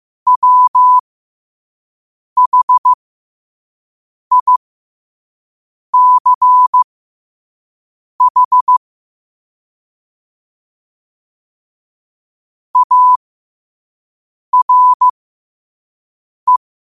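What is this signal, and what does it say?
Morse code, heard as a keyed tone, about 15 wpm, spaced as for 5 wpm, 1000 Hz. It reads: WHICH ARE
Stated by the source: Morse code practice at 5 wpm.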